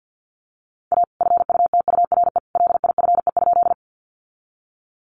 Morse code, heard as a keyed tone, about 50 wpm, 700 Hz. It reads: A FUTURE LIFE?
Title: Sentences from Top 700 Words 50wpm